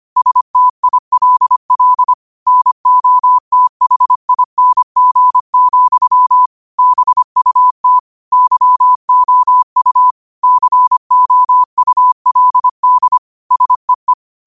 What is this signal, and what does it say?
Morse code, heard as a keyed tone, about 25 wpm, 1000 Hz. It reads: STILL NOTHING, BUT YOU COULD SEE
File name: qotdfiles / 25wpm